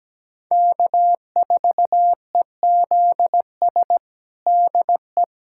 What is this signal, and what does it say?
Morse code, heard as a keyed tone, about 17 wpm, 700 Hz. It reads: K4EZS DE
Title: USA Callsign DE Callsign - 5 Characters - 17wpm